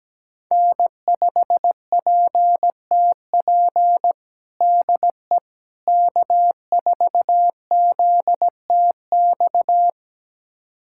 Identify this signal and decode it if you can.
Morse code, heard as a keyed tone, about 17 wpm, 700 Hz. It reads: N5PTP DE K4ZTX
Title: USA Callsign DE Callsign - 5 Characters - 17wpm